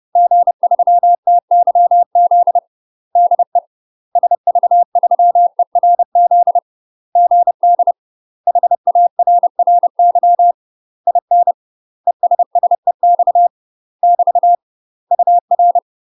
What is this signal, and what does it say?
Morse code, heard as a keyed tone, about 30 wpm, 700 Hz. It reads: G3TYZ DE SV3ERZ GD HARRY IN ESSEX <BT> UR
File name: UK MASTER QSOs TRAINING / 30WPM / UK MASTER QSOs 0047